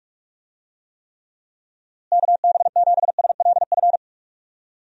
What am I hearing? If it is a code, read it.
Morse code, heard as a keyed tone, about 46 wpm, 700 Hz. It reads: KB7SLF
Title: USA Call Signs - 6 Characters - Repeated 2x - 45wpm